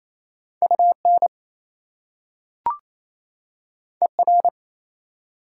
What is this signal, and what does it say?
Morse code, heard as a keyed tone, about 28 wpm, 690 Hz. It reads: UN ER
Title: Prefixes and Suffixes 28wpm